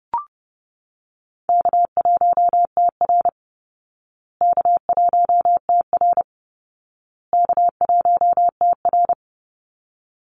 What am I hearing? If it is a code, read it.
Morse code, heard as a keyed tone, about 30 wpm, 700 Hz. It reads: K1TR K1TR K1TR
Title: Easy Call Signs Repeated 3x - 30wpm